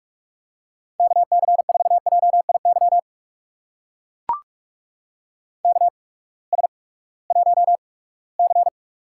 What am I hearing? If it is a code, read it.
Morse code, heard as a keyed tone, about 44 wpm, spaced as for 22 wpm, 700 Hz. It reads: KC4JIY KS1C
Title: USA Call Signs - 8x Character Spacing - 45wpm